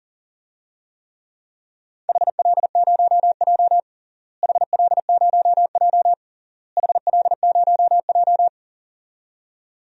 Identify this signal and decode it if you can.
Morse code, heard as a keyed tone, about 40 wpm, 700 Hz. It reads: HL0J HL0J HL0J
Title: DXCC-100 Call Signs - 4 Characters - Repeated 3x - 40wpm